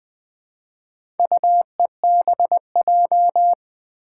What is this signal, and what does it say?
Morse code, heard as a keyed tone, about 20 wpm, 700 Hz. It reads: UEBJ
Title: Four Challenging Letter-Number Combinations 20wpm